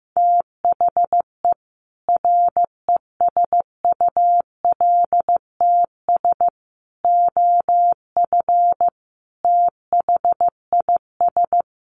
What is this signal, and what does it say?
Morse code, heard as a keyed tone, about 15 wpm, 700 Hz. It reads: THE RESULTS OF THIS
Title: Sets of 5 Words 15wpm